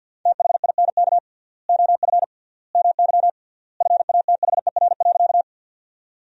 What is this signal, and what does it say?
Morse code, heard as a keyed tone, about 52 wpm, 700 Hz. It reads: THINK OF MY FATHER.
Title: 3x Word-Spacing - Sentences from Top 300 Words - 50wpm